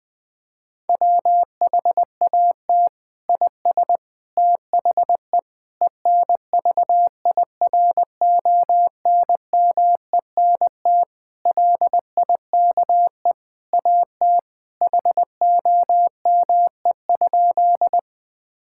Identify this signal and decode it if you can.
Morse code, heard as a keyed tone, about 20 wpm, 700 Hz. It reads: WHAT IS THE ENVIRONMENT LIKE AT HOME?
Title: Sentences from Top 1000 Words - Review Part 5 - 20wpm